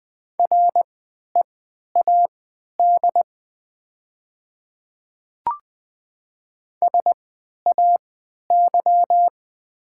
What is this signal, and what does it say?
Morse code, heard as a keyed tone, about 20 wpm, spaced as for 11 wpm, 700 Hz. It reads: READ SAY
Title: 3x Character Spacing - Top 200 Words - 20wpm